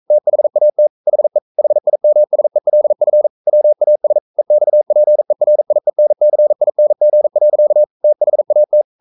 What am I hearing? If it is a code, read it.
Morse code, heard as a keyed tone, about 42 wpm, 600 Hz. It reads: THAT HE HIMSELF WAS EXPERIENCING. THAT
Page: Romanen Never af Ken Follet på engelsk.